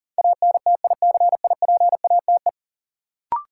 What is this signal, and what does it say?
Morse code, heard as a keyed tone, about 40 wpm, 700 Hz. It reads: ANTICIPATE
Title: Sentences from Top 1500 Words - New Word Drill - 40wpm